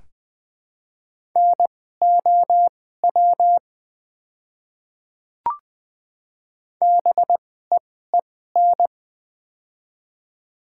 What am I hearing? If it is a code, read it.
Morse code, heard as a keyed tone, about 20 wpm, spaced as for 14 wpm, 700 Hz. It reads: NOW BEEN
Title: 2x Character Spacing - Top 100 Words - 20wpm